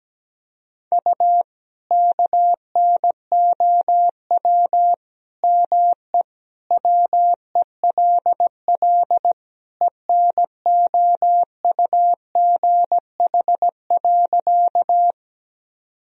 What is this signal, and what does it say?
Morse code, heard as a keyed tone, about 17 wpm, 700 Hz. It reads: U KNOW ME WELL ENOUGH.
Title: Sentences from Top 400 Words and Abbreviations 17wpm